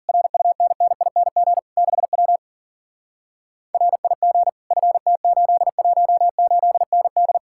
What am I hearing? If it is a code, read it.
Morse code, heard as a keyed tone, about 39 wpm, 700 Hz. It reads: RUNNING 6W RIG FT818ND
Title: Speed-Racing Common QSO Phrases - 40wpm